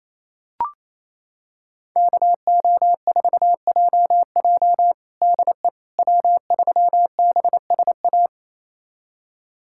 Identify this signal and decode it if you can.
Morse code, heard as a keyed tone, about 28 wpm, 700 Hz. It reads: KO4JJ DE W3BSA
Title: USA Callsign DE Callsign - 5 Characters - 28wpm